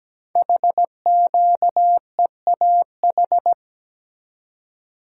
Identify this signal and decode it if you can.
Morse code, heard as a keyed tone, about 17 wpm, 700 Hz. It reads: HQEAH
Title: Code Groups - 5 Letters - 17wpm